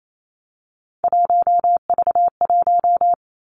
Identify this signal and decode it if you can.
Morse code, heard as a keyed tone, about 28 wpm, 700 Hz. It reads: 1V1
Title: Three Challenging Letter-Number Combinations 28wpm